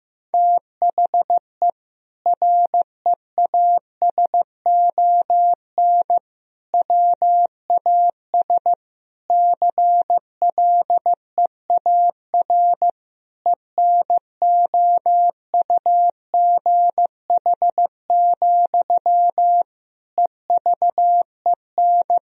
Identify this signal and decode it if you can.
Morse code, heard as a keyed tone, about 15 wpm, 700 Hz. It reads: THE REASON WAS CLEAR ENOUGH, EVEN
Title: Sentences from Top 1000 Words - Review Part 2 - 15wpm